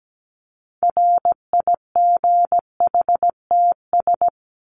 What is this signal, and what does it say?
Morse code, heard as a keyed tone, about 17 wpm, 700 Hz. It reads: RIGHTS
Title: ES:S Suffix - Top 100 Words with Base Words 17wpm